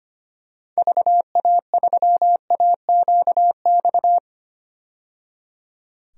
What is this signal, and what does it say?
Morse code, heard as a keyed tone, about 25 wpm, 700 Hz. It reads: VA3AQX
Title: DXCC-100 Call Signs - 6 Characters - 25wpm